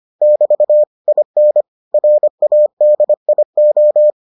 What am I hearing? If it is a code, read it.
Morse code, heard as a keyed tone, about 25 wpm, 600 Hz. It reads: <BT> IN RADIO